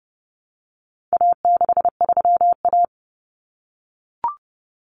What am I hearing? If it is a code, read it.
Morse code, heard as a keyed tone, about 30 wpm, 700 Hz. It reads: A63A
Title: Common Cut Numbers - 4-Digit 30wpm